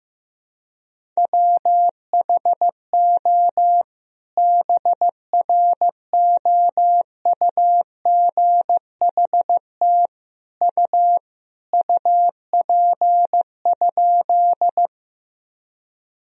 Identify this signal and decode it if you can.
Morse code, heard as a keyed tone, about 15 wpm, 700 Hz. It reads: WHO BROUGHT U UP?
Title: Sentences from Top 500 Words and Abbreviations 15wpm